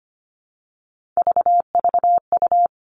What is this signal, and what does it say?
Morse code, heard as a keyed tone, about 25 wpm, 700 Hz. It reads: VVU
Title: All Cut Numbers - 3-Digit 25wpm